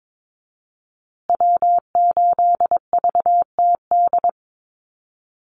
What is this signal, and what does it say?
Morse code, heard as a keyed tone, about 22 wpm, 700 Hz. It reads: W8VTD
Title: USA Call Signs Repeated 3x 22wpm